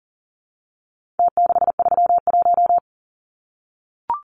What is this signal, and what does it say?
Morse code, heard as a keyed tone, about 40 wpm, 700 Hz. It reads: T631
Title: Numbers - Contest Style - 4-Digit 40wpm